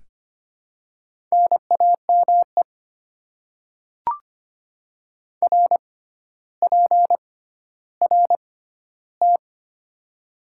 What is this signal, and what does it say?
Morse code, heard as a keyed tone, about 25 wpm, spaced as for 13 wpm, 690 Hz. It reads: NAME RPRT
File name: Top 90 QSO Elements - 6x Character Spacing - 25wpm